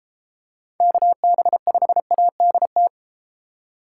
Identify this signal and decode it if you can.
Morse code, heard as a keyed tone, about 32 wpm, 700 Hz. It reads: KB5ADT